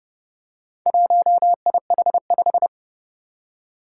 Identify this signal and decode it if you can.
Morse code, heard as a keyed tone, about 30 wpm, 700 Hz. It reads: 1IH5
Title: Four Challenging Letter-Number Combinations 30wpm